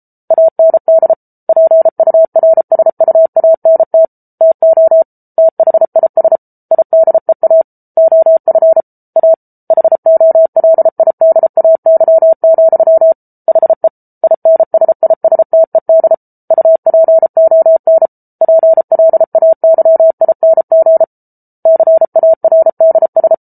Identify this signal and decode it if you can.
Morse code, heard as a keyed tone, about 34 wpm, 650 Hz. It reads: AND PURSUANT TO THIS IDEA OF A HOLIDAY, HE INSISTED UPON PLAYING CARDS